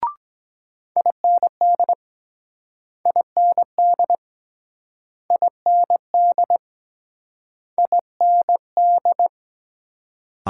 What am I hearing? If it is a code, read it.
Morse code, heard as a keyed tone, about 24 wpm, 700 Hz. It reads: IND IND IND IND